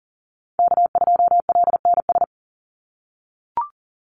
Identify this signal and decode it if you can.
Morse code, heard as a keyed tone, about 40 wpm, 700 Hz. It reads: K2LNS
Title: Call Signs - 40wpm